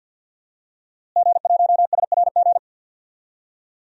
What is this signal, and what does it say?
Morse code, heard as a keyed tone, about 52 wpm, 700 Hz. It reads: G1SRG
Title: DXCC-100 Call Signs - 5 Characters - Repeated 2x - 50wpm